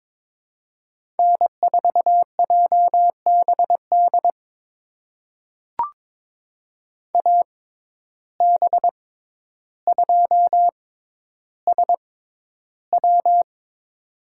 Morse code, heard as a keyed tone, about 22 wpm, spaced as for 11 wpm, 700 Hz. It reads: N4JBD AB2SW
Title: USA Call Signs - 5 Characters - 6x Character Spacing - 22wpm